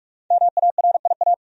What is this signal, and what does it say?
Morse code, heard as a keyed tone, about 47 wpm, 700 Hz. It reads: MARIA
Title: USA Common Names - Max 5 Letters - 45wpm